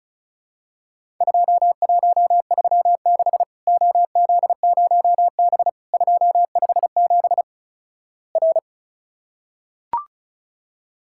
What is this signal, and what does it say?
Morse code, heard as a keyed tone, about 36 wpm, 700 Hz. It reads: 2136 OZ0B 257 R
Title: Worked All Europe DX Contest - QTC Exchange - 35wpm